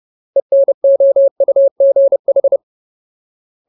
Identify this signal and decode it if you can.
Morse code, heard as a keyed tone, about 30 wpm, 550 Hz. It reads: ENOUGH
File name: Ultimate Word List / 30WPM Ultimate